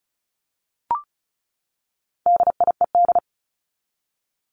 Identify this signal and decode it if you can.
Morse code, heard as a keyed tone, about 34 wpm, 700 Hz. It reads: DIED
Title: ED Suffix - Top 100 Words with Base Words 35wpm